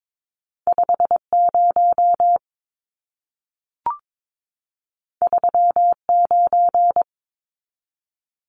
Morse code, heard as a keyed tone, about 22 wpm, 700 Hz. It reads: 50 39
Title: Numbers - 2-Digit 22wpm